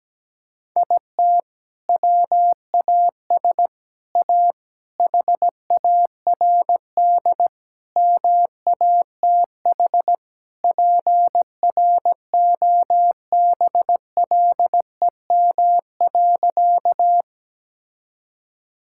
Morse code, heard as a keyed tone, about 17 wpm, 700 Hz. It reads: IT WAS A HARD MATH PROBLEM.
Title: Sentences from Top 1000 Words - Review Part 6 - 17wpm